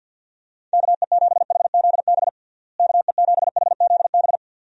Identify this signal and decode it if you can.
Morse code, heard as a keyed tone, about 52 wpm, 700 Hz. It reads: KE7HZB KE7HZB
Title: USA Call Signs - 6 Characters - Repeated 2x - 50wpm